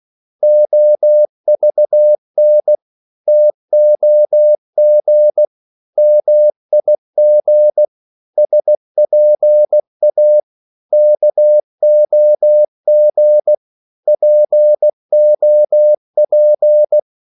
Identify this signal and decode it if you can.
Morse code, heard as a keyed tone, about 16 wpm, 600 Hz. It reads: OVN TOG MIG SPA KOG POP
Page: Korte danske ord 16wpm